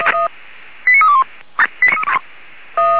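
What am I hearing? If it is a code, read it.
Five-tone selective call (ZVEI), heard as quick runs of five tones, tones of about 70 ms. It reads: 89321
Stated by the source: ZVEI five-tone selective call 89319, received by SDR.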